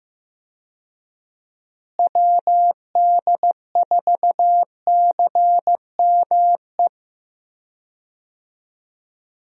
Morse code, heard as a keyed tone, about 15 wpm, 700 Hz. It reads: WD4CME
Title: USA Call Signs - 6 Characters - Repeated 3x - 15wpm